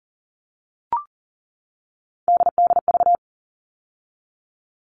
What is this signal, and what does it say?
Morse code, heard as a keyed tone, about 40 wpm, 700 Hz. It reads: DDV
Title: All Cut Numbers - 3-Digit 40wpm